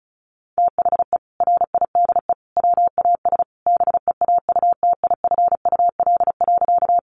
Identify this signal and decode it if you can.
Morse code, heard as a keyed tone, about 34 wpm, 700 Hz. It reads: THE RIDE WAS BEAUTIFUL.
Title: Sentences from Top 1200 Words 35wpm